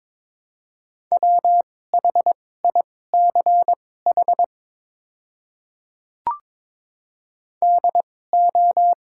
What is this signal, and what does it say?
Morse code, heard as a keyed tone, about 22 wpm, spaced as for 16 wpm, 700 Hz. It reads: WHICH DO
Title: 2x Character Spacing - Top 100 Words - 22wpm